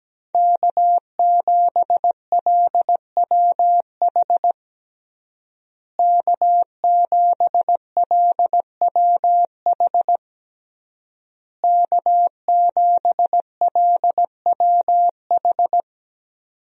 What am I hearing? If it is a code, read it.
Morse code, heard as a keyed tone, about 17 wpm, 700 Hz. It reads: K7LWH K7LWH K7LWH
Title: USA Call Signs - 5 Characters - Repeated 3x - 17wpm